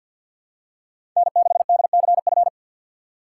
Morse code, heard as a keyed tone, about 47 wpm, 700 Hz. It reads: N6DCF
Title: USA Call Signs - 5 Characters - 4x Character Spacing - 50wpm